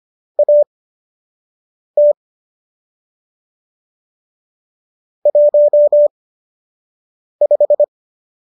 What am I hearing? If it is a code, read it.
Morse code, heard as a keyed tone, about 25 wpm, spaced as for 6 wpm, 590 Hz. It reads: AT 15
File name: qso012 06